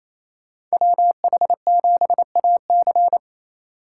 Morse code, heard as a keyed tone, about 36 wpm, 700 Hz. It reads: WH7AC